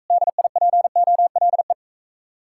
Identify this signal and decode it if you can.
Morse code, heard as a keyed tone, about 44 wpm, 700 Hz. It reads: DIPOLE